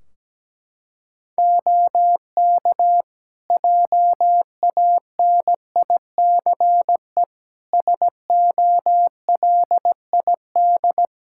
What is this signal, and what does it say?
Morse code, heard as a keyed tone, about 17 wpm, 700 Hz. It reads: OK JANICE SOLID
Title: Practice QSO - Exchange 3 - More Info - 17wpm